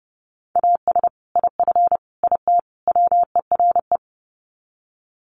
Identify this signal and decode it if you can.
Morse code, heard as a keyed tone, about 30 wpm, 700 Hz. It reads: AS IF IT WERE
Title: Sets of 4 Words 30wpm